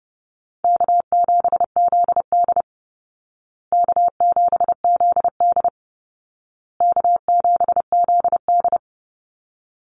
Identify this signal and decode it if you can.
Morse code, heard as a keyed tone, about 30 wpm, 700 Hz. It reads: K7ZD K7ZD K7ZD
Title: Easy Call Signs Repeated 3x - 30wpm